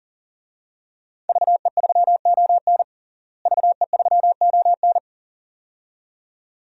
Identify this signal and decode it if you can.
Morse code, heard as a keyed tone, about 37 wpm, 700 Hz. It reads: VE3ON VE3ON
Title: DXCC-100 Call Signs - 5 Characters - Repeated 2x - 40wpm